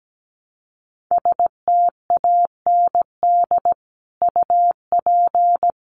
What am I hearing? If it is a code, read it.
Morse code, heard as a keyed tone, about 17 wpm, 700 Hz. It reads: STAND UP
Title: Sentences from Top 1000 Words 17wpm